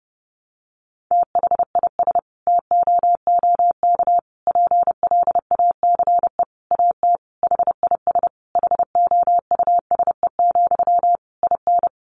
Morse code, heard as a keyed tone, about 30 wpm, 700 Hz. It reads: THIS TOOK PLACE AT HIS HOUSE, IN
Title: Sentences from Top 300 Words 30wpm